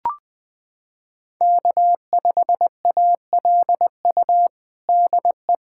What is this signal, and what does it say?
Morse code, heard as a keyed tone, about 20 wpm, 700 Hz. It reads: K5ALU DE
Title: USA Callsign DE Callsign - 5 Characters - 20wpm